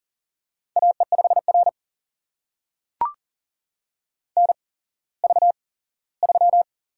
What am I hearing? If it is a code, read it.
Morse code, heard as a keyed tone, about 38 wpm, spaced as for 19 wpm, 690 Hz. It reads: AE5R NV3
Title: USA Call Signs - 4 Characters - 8x Character Spacing - 40wpm